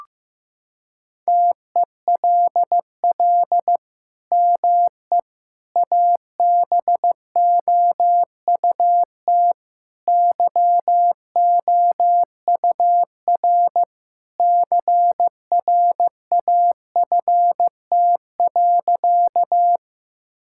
Morse code, heard as a keyed tone, about 15 wpm, 700 Hz. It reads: TELL ME ABOUT YOUR CRAFT.
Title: Sentences from Top 1000 Words - Review Part 6 - 15wpm